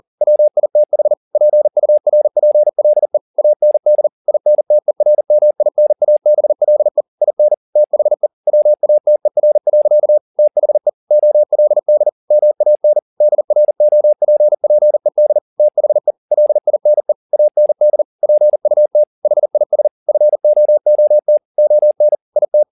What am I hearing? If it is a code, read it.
Morse code, heard as a keyed tone, about 40 wpm, 600 Hz. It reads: WITH PURPLE AND INTERMINABLE IN THE WATER. THE OLD MAN DROPPED THE LINE AND PUT HIS FOOT ON IT
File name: the old man and the sea 2 40wpm